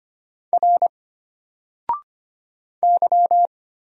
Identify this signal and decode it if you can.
Morse code, heard as a keyed tone, about 25 wpm, 700 Hz. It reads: R Y
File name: Single-Letter-Number-25wpm